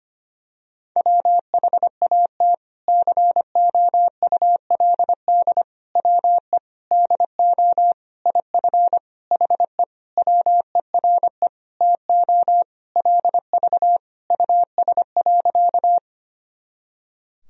Morse code, heard as a keyed tone, about 25 wpm, 700 Hz. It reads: WHAT COULD WE DO IF HE WERE TO LV US.
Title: Sentences from Top 400 Words and Abbreviations 25wpm